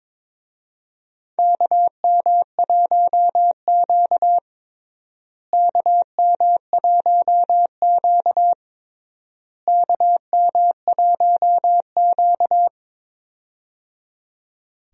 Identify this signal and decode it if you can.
Morse code, heard as a keyed tone, about 22 wpm, 700 Hz. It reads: KM1Q KM1Q KM1Q
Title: USA Call Signs - 4 Characters - Repeated 3x - 22wpm